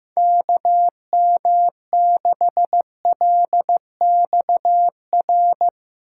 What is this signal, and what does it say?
Morse code, heard as a keyed tone, about 16 wpm, 700 Hz. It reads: KM6LXR
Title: Speed Racing - USA 2x3 Call Signs 15wpm